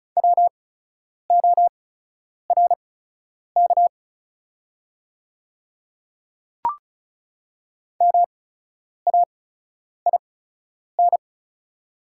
Morse code, heard as a keyed tone, about 36 wpm, spaced as for 10 wpm, 690 Hz. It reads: WORK MAIN
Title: 8x Character Spacing - Top 500 Words - 35wpm